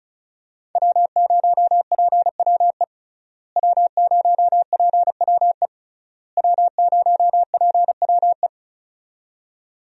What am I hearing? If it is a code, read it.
Morse code, heard as a keyed tone, about 36 wpm, 700 Hz. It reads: W0PWE W0PWE W0PWE
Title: USA Call Signs - 5 Characters - Repeated 3x - 35wpm